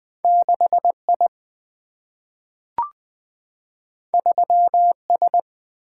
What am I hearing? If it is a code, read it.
Morse code, heard as a keyed tone, about 20 wpm, 700 Hz. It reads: 6I 3S
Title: Single-Letter Single-Number 20wpm